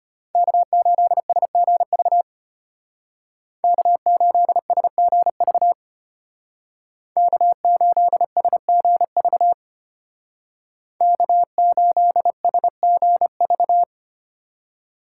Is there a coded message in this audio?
Yes: Morse code from 0.3 to 13.8 s, heard as a keyed tone, about 32 wpm, 700 Hz. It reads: K8SGV K8SGV K8SGV K8SGV